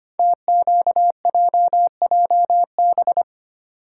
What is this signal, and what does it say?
Morse code, heard as a keyed tone, about 25 wpm, 700 Hz. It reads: TQJJB
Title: Code Groups - 5 Letter:Numbers - 25wpm